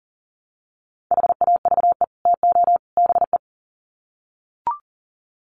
Morse code, heard as a keyed tone, about 38 wpm, 700 Hz. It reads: HAVE TO BE
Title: Sets of 3 Words 40wpm